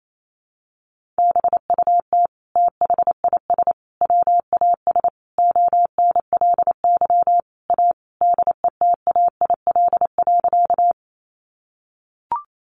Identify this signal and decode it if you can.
Morse code, heard as a keyed tone, about 28 wpm, 700 Hz. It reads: BUT THIS WAS ONLY A DETAIL.
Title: Sentences from Top 1500 Words 28wpm